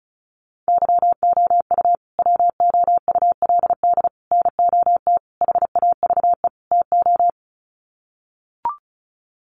Morse code, heard as a keyed tone, about 34 wpm, 700 Hz. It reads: YOU WOULD NOT HAVE TO
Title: Sets of 5 Words Encore 35wpm